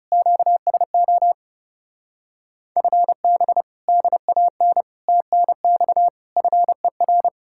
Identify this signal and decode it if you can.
Morse code, heard as a keyed tone, about 40 wpm, 700 Hz. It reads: QSO FB DAN TNX FER